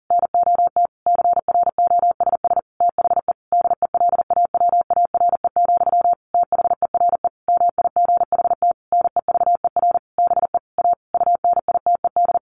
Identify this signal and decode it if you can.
Morse code, heard as a keyed tone, about 40 wpm, 700 Hz. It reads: NOT CROSS THE DELAWARE, THERE MIGHT NEVER BE A UNITED
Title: A Sound Of Thunder 40wpm